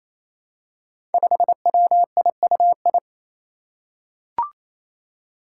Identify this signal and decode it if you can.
Morse code, heard as a keyed tone, about 28 wpm, 700 Hz. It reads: 5WIUI
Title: Five Challenging Letter-Number Combinations 28wpm